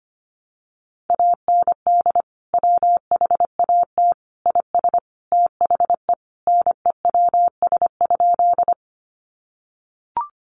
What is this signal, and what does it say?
Morse code, heard as a keyed tone, about 25 wpm, 700 Hz. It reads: AND WHAT IS THE NEWS?
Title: Sentences from Top 800 Words 25wpm